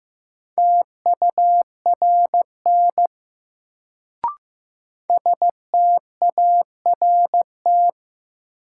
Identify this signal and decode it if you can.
Morse code, heard as a keyed tone, about 15 wpm, 700 Hz. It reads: TURN START
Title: Sets of 4 Words - New Word Drill - 15wpm